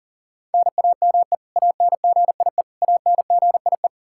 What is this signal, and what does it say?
Morse code, heard as a keyed tone, about 40 wpm, 700 Hz. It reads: NAME ANGIE ANGIE
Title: K1USN SST QSO Practice 40wpm